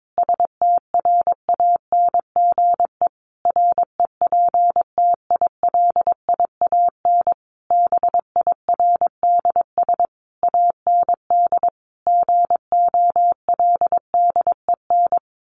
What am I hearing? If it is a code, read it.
Morse code, heard as a keyed tone, about 22 wpm, 700 Hz. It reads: STRANGE REPTILIAN BIRDS AND GOLDEN
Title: A Sound Of Thunder 22wpm